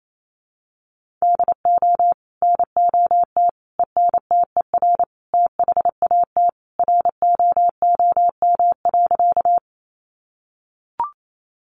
Morse code, heard as a keyed tone, about 28 wpm, 700 Hz. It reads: DO NOT ENTER THAT ROOM.
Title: Sentences from Top 1300 Words 28wpm